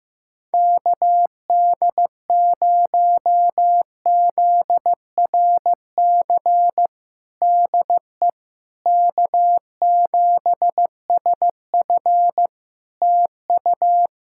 Morse code, heard as a keyed tone, about 15 wpm, 700 Hz. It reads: KD0ZRC DE K7SF TU
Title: Practice QSO - Exchange 4 - Ending - 15wpm